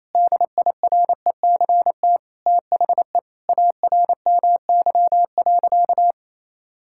Morse code, heard as a keyed tone, about 28 wpm, 700 Hz. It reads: DIRECT THE ARMY.
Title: Sentences from Top 1000 Words - Review Part 5 - 28wpm